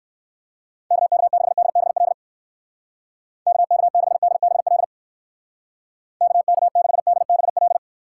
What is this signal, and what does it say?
Morse code, heard as a keyed tone, about 58 wpm, 700 Hz. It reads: KK6DBL KK6DBL KK6DBL